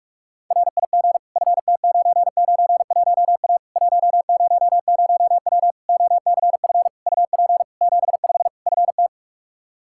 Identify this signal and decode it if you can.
Morse code, heard as a keyed tone, about 46 wpm, 700 Hz. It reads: RIG FT991A 100W OCF UP 75 FT
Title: K1USN SST QSO Practice 45wpm